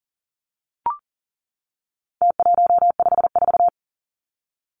Morse code, heard as a keyed tone, about 40 wpm, 700 Hz. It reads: T154
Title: Numbers - Contest Style - 4-Digit 40wpm